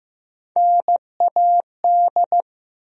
Morse code, heard as a keyed tone, about 15 wpm, 700 Hz. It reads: NAD
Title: USA Call Sign Postfixes - 3 Letters 15wpm